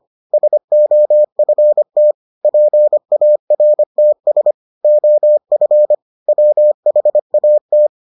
Morse code, heard as a keyed tone, about 25 wpm, 600 Hz. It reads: SOFT PARTS OF WHAT